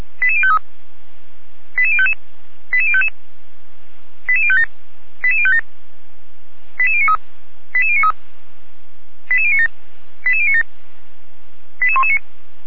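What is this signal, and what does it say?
Five-tone selective call (ZVEI), heard as quick runs of five tones, tones of about 80 ms. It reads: 88053 88055 88055 88057 88057 88093 88093 88097 88097 88119
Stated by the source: Five-tone ZVEI selective call received by SDR, from the audio server of Feuerwehr Orlamünde (fire brigade).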